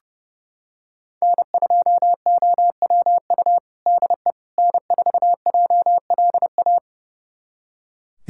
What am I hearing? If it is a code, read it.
Morse code, heard as a keyed tone, about 30 wpm, 700 Hz. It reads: N2OWU DE N4JLA
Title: USA Callsign DE Callsign - 5 Characters - 30wpm